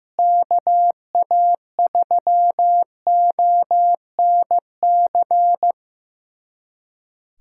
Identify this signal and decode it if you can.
Morse code, heard as a keyed tone, about 15 wpm, 700 Hz. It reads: KA3ONC
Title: USA Callsign DE Callsign 15wpm